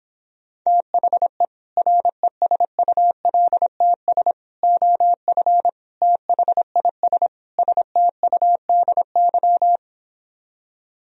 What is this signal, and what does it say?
Morse code, heard as a keyed tone, about 29 wpm, 700 Hz. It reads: THE RESULTS OF THIS STUDY